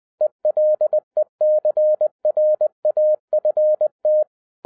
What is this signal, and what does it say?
Morse code, heard as a keyed tone, about 20 wpm, 610 Hz. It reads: ELECRAFT
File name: 2011-07-mp3 / n23 QRP OTAKU 20wpm